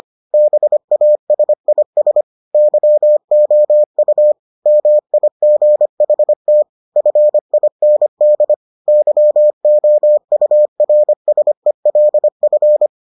Morse code, heard as a keyed tone, about 25 wpm, 600 Hz. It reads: BASIS YOU MIGHT FIND YOURSELF